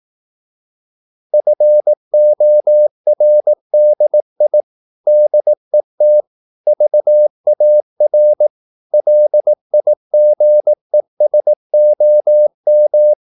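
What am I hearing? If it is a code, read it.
Morse code, heard as a keyed tone, about 18 wpm, 600 Hz. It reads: FORDI DET VAR LIGESOM
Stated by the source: Slow (QRS) Morse practice at 18 wpm, CW med Gnister.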